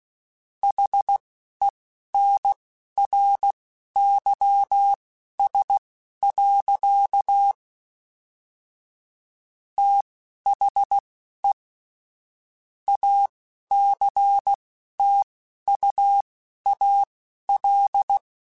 Morse code, heard as a keyed tone, about 16 wpm, spaced as for 11 wpm, 800 Hz. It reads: HENRYS. THE ACTUAL